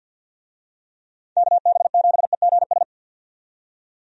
Morse code, heard as a keyed tone, about 52 wpm, 700 Hz. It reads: KB7EGS
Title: USA Call Signs - 6 Characters - 50wpm